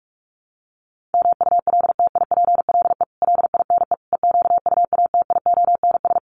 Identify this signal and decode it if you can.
Morse code, heard as a keyed tone, about 44 wpm, 700 Hz. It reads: MULTIPLE LINE EQUATIONS
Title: Sets of 5 Words 45wpm